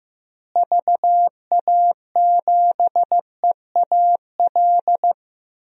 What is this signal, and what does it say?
Morse code, heard as a keyed tone, about 15 wpm, 700 Hz. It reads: VA7EAL
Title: DXCC-100 Call Signs - 6 Characters - Repeated 3x - 15wpm